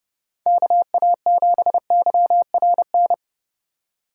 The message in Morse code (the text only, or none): KA7YRN